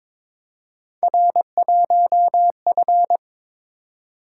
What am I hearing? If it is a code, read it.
Morse code, heard as a keyed tone, about 22 wpm, 700 Hz. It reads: R1F
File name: practice / 22wpm / Three Letter-Number Combinations - 22wpm